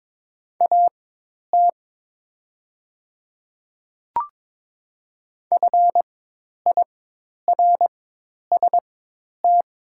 Morse code, heard as a keyed tone, about 22 wpm, spaced as for 10 wpm, 700 Hz. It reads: AT FIRST
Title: 4x Character Spacing - Top 300 Words - 22wpm